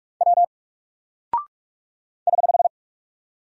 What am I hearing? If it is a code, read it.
Morse code, heard as a keyed tone, about 45 wpm, 700 Hz. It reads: W <HH>
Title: Single Character (Letter:Number:Punctuation:Prosign) Alt 45wpm